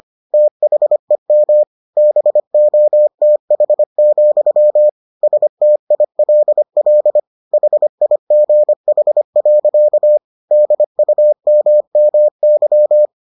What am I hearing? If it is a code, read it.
Morse code, heard as a keyed tone, about 25 wpm, 600 Hz. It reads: THEM BOTH, STILL HIGH. DUMMY